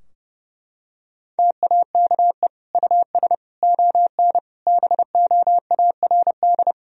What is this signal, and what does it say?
Morse code, heard as a keyed tone, about 32 wpm, 700 Hz. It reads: TAKE US ON BOARD
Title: Sentences from Top 1000 Words - Review Part 4 - 30wpm